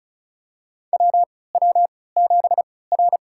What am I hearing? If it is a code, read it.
Morse code, heard as a keyed tone, about 34 wpm, spaced as for 20 wpm, 700 Hz. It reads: WW7R
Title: USA Call Signs - 4 Characters - 3x Character Spacing - 35wpm